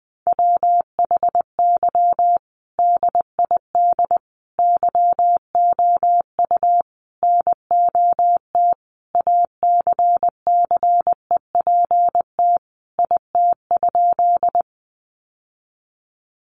Morse code, heard as a keyed tone, about 20 wpm, 700 Hz. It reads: WHY DID YOU NOT ACCEPT IT?
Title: Sentences from Top 1300 Words 20wpm